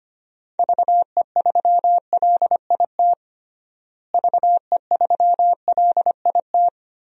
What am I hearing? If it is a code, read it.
Morse code, heard as a keyed tone, about 25 wpm, 700 Hz. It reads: VE3LIT VE3LIT
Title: DXCC-100 Call Signs - 6 Characters - Repeated 3x - 25wpm